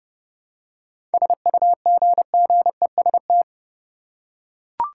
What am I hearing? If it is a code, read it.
Morse code, heard as a keyed tone, about 32 wpm, 700 Hz. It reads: SUGGEST
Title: Seven Letter Words - 30wpm